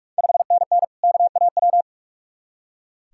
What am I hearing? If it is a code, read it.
Morse code, heard as a keyed tone, about 41 wpm, 700 Hz. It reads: 5NN KAW
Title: Ontario QSO Party Exchange 45wpm